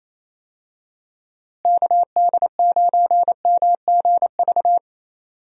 Morse code, heard as a keyed tone, about 27 wpm, 700 Hz. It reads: KD9MGV
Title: USA Call Signs - 6 Characters - Repeated 2x - 28wpm